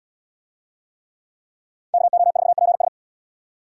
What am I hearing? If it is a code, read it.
Morse code, heard as a keyed tone, about 77 wpm, 700 Hz. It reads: KO4PS